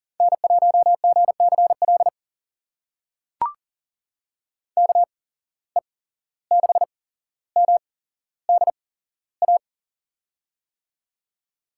Morse code, heard as a keyed tone, about 42 wpm, spaced as for 16 wpm, 700 Hz. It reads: N1GCL KE6MDA
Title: USA Call Signs - 8x Character Spacing - 40wpm